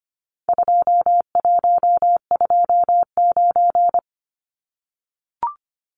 Morse code, heard as a keyed tone, about 25 wpm, 700 Hz. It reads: 2129
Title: Numbers - 4-Digit 25wpm